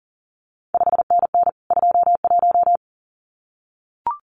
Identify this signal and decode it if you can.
Morse code, heard as a keyed tone, about 41 wpm, 700 Hz. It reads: 5NN 21
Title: CQ WW DX Contest Exchange Encore - 40wpm